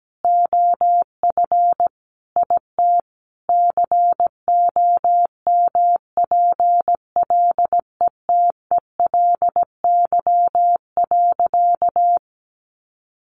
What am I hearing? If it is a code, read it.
Morse code, heard as a keyed tone, about 17 wpm, 700 Hz. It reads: OF IT COMPLETELY.
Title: Sentences from Top 1100 Words 17wpm